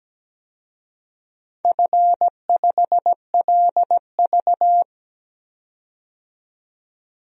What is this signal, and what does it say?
Morse code, heard as a keyed tone, about 17 wpm, 700 Hz. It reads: F5LV
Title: DXCC-100 Call Signs - 4 Characters - 17wpm